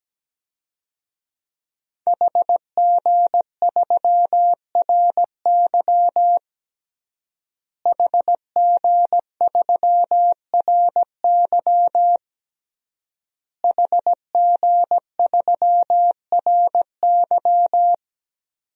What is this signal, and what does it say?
Morse code, heard as a keyed tone, about 17 wpm, 700 Hz. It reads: HG3RY HG3RY HG3RY
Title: DXCC-100 Call Signs - 5 Characters - Repeated 3x - 17wpm